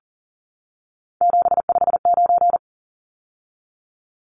Morse code, heard as a keyed tone, about 40 wpm, 700 Hz. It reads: 759
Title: Numbers - 3-Digit 40wpm